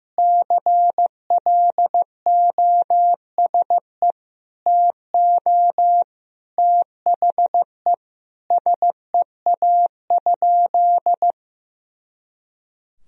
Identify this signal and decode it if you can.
Morse code, heard as a keyed tone, about 15 wpm, 700 Hz. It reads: CLOSE TO THE SEA?
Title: Sentences from Top 300 Words and Abbreviations 15wpm